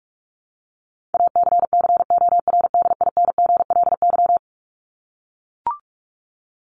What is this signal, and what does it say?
Morse code, heard as a keyed tone, about 44 wpm, 700 Hz. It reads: ACCORDINGLY
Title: LY Suffix - Top 100 Words with Base Words 45wpm